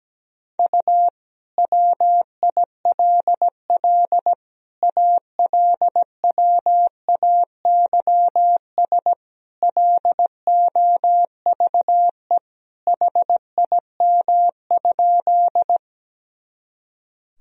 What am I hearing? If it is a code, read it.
Morse code, heard as a keyed tone, about 17 wpm, 700 Hz. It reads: U WILL ALWAYS LOVE HIM?
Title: Sentences from Top 400 Words and Abbreviations 17wpm